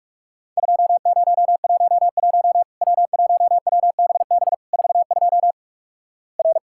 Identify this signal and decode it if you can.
Morse code, heard as a keyed tone, about 46 wpm, 700 Hz. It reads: 2011 W1WBB 42 R
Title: Worked All Europe DX Contest - QTC Exchange - 45wpm